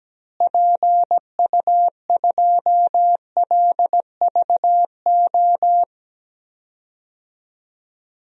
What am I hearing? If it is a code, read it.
Morse code, heard as a keyed tone, about 17 wpm, 700 Hz. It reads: PU2LVO
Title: DXCC-100 Call Signs - 6 Characters - Repeated 3x - 17wpm